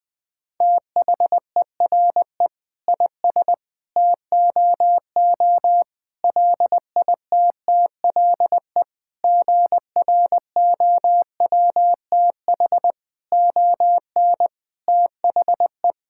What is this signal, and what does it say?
Morse code, heard as a keyed tone, about 20 wpm, 700 Hz. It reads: THERE IS TOO LITTLE GROWTH ON THE
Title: Sentences from Top 1000 Words - Review Part 5 - 20wpm